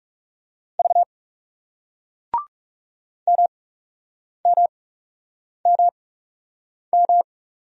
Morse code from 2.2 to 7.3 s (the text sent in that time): MMMM